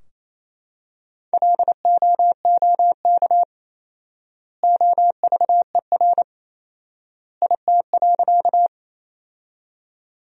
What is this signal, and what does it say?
Morse code, heard as a keyed tone, about 28 wpm, 700 Hz. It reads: LOOK OVER IT.
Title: 4x Word-Spacing - Sentences from Top 100 Words - 28wpm